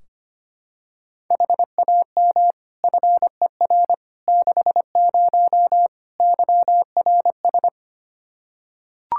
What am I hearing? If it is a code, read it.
Morse code, heard as a keyed tone, about 25 wpm, 700 Hz. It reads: HAM FER 60 YRS
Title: Practice QSO - Exchange 3 - More Info - 25wpm